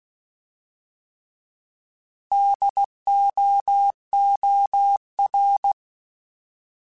Morse code, heard as a keyed tone, about 16 wpm, 800 Hz. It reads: DOOR
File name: Alphabetical / Alphabetical mp3s / Ds @15